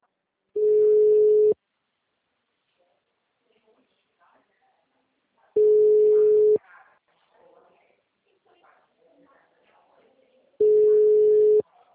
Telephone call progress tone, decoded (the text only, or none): ringback tone